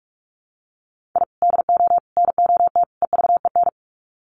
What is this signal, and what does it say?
Morse code, heard as a keyed tone, about 44 wpm, 700 Hz. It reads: I DO NOT EVEN
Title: Sets of 4 Words 45wpm